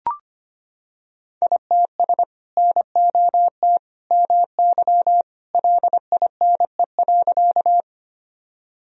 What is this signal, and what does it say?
Morse code, heard as a keyed tone, about 25 wpm, 700 Hz. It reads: ITS NOT MY LINE.
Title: Sentences from Top 1000 Words - Review Part 1 - 25wpm